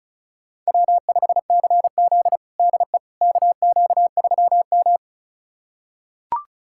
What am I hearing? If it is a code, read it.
Morse code, heard as a keyed tone, about 36 wpm, 700 Hz. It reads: W5CZ DE KQ3M
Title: USA Callsign DE Callsign - 4 Characters - 35wpm